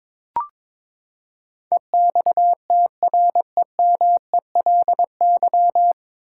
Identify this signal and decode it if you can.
Morse code, heard as a keyed tone, about 22 wpm, 700 Hz. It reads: EXTREMELY
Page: Morse Code Ninja practice 22wpm